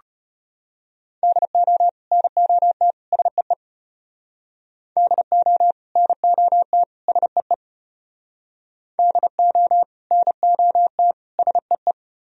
Morse code, heard as a keyed tone, about 33 wpm, 700 Hz. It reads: DO NOT SEE DO NOT SEE DO NOT SEE